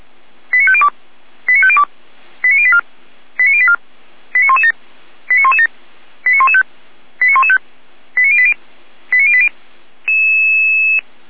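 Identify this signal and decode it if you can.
Five-tone selective call (ZVEI), heard as quick runs of five tones, tones of about 70 ms. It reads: 89502 89502 89084 89084 89117 89117 89115 89115 89080 89080
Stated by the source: ZVEI five-tone selective call 89094 received by SDR.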